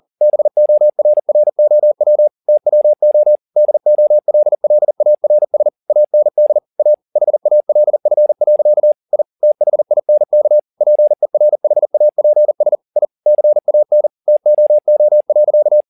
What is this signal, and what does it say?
Morse code, heard as a keyed tone, about 40 wpm, 600 Hz. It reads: BORROW TWO DOLLARS AND A HALF. I THINK PERHAPS I CAN TOO.